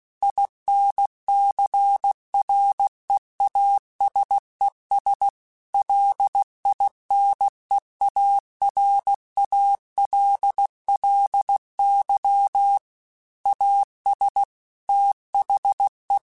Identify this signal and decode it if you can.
Morse code, heard as a keyed tone, about 16 wpm, 800 Hz. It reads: INCREASES LINEARALLY AS THE
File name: Technical Subjects / Technical Subjects-mp3s / XL @15